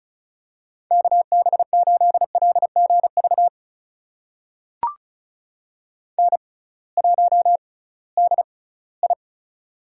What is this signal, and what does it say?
Morse code, heard as a keyed tone, about 36 wpm, spaced as for 20 wpm, 700 Hz. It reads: KB8LGV N1DI
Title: USA Call Signs - 6x Character Spacing - 35wpm